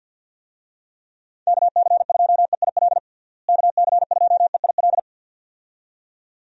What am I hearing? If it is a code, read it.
Morse code, heard as a keyed tone, about 47 wpm, 700 Hz. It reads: KC2EIL KC2EIL
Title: USA Call Signs - 6 Characters - Repeated 2x - 50wpm